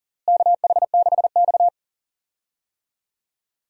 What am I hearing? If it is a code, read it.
Morse code, heard as a keyed tone, about 40 wpm, 700 Hz. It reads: KH6X
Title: DXCC-100 Call Signs - 4 Characters - Repeated 2x - 40wpm